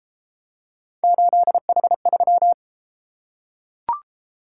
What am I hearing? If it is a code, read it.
Morse code, heard as a keyed tone, about 31 wpm, 700 Hz. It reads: 8H3